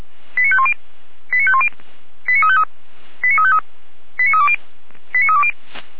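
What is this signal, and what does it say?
Five-tone selective call (ZVEI), heard as quick runs of five tones, tones of about 70 ms. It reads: 89510 89510 89353 89353 89320 89320
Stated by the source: ZVEI five-tone call received on an SDR.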